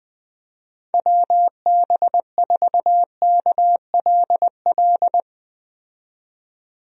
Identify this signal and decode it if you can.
Morse code, heard as a keyed tone, about 20 wpm, 700 Hz. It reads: WB4KLL
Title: USA Call Signs - 6 Characters - 20wpm